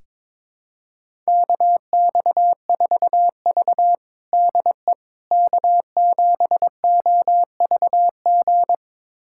Morse code, heard as a keyed tone, about 22 wpm, 700 Hz. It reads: KX4V DE K7OVG
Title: USA Callsign DE Callsign 22wpm